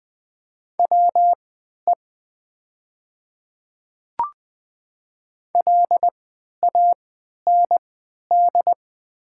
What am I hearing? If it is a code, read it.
Morse code, heard as a keyed tone, about 20 wpm, spaced as for 11 wpm, 700 Hz. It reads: WE LAND
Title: 3x Character Spacing - Top 200 Words - 20wpm